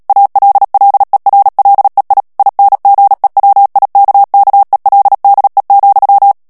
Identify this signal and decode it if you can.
Morse code, heard as a keyed tone, about 37 wpm, 800 Hz. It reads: ALLERLEI INGEWIKKELDE,
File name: morse mp3 / HOLLANDAIS texte clair 2 vitesse 35 mots minute